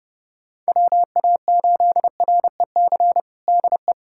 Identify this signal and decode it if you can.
Morse code, heard as a keyed tone, about 30 wpm, 700 Hz. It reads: WA8REC DE
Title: USA Callsign DE Callsign - 6 Characters - 30wpm